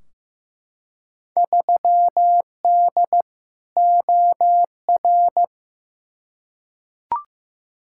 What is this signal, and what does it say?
Morse code, heard as a keyed tone, about 15 wpm, 700 Hz. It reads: 3D OR
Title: Field Day Exchanges Encore - 15wpm